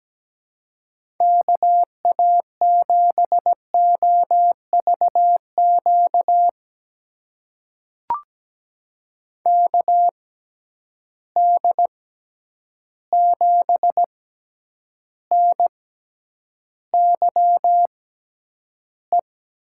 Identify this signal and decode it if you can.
Morse code, heard as a keyed tone, about 17 wpm, spaced as for 9 wpm, 700 Hz. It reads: KA7OVQ KD7NYE
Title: USA Call Signs - 6 Characters - 6x Character Spacing - 17wpm